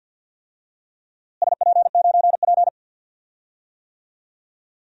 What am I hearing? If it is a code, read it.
Morse code, heard as a keyed tone, about 47 wpm, 700 Hz. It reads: SP9P
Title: DXCC-100 Call Signs - 4 Characters - Repeated 2x - 50wpm